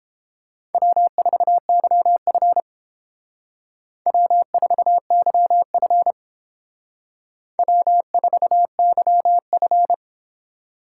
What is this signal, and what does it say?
Morse code, heard as a keyed tone, about 29 wpm, 700 Hz. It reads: W4YF W4YF W4YF